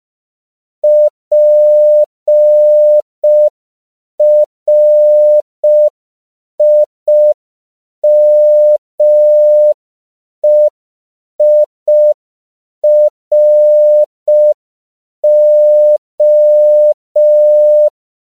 Morse code, heard as a keyed tone, about 5 wpm, 600 Hz. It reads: PRIMEIRO
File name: CW / 5PPM